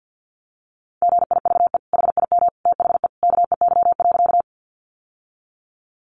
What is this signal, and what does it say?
Morse code, heard as a keyed tone, about 50 wpm, 700 Hz. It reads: GIVE HIM THE KEY.
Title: Sentences from Top 600 Words 50wpm